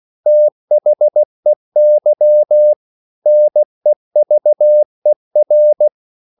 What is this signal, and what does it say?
Morse code, heard as a keyed tone, about 16 wpm, 600 Hz. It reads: THEY NEVER